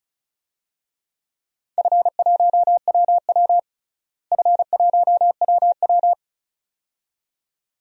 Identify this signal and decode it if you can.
Morse code, heard as a keyed tone, about 34 wpm, 700 Hz. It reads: F1WW F1WW
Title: DXCC-100 Call Signs - 4 Characters - Repeated 2x - 35wpm